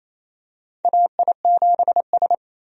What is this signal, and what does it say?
Morse code, heard as a keyed tone, about 28 wpm, 700 Hz. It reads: AI7S
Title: USA Call Signs - 4 Characters - 28wpm